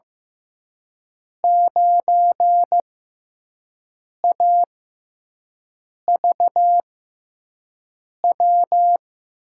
Morse code, heard as a keyed tone, about 15 wpm, spaced as for 5 wpm, 700 Hz. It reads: 9AVW
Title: USA Call Signs - 5 Characters - 6x Character Spacing - 15wpm